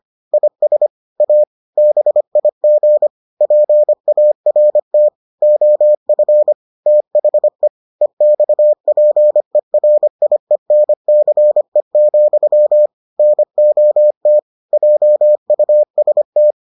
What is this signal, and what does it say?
Morse code, heard as a keyed tone, about 25 wpm, 600 Hz. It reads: IS A BIG PART OF THE EXPERIENCE, NOT JUST